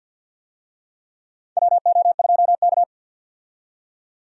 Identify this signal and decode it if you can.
Morse code, heard as a keyed tone, about 50 wpm, 700 Hz. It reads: WO2K
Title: USA Call Signs - 4 Characters - 50wpm